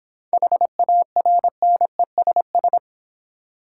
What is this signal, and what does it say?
Morse code, heard as a keyed tone, about 26 wpm, 700 Hz. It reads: HARNESS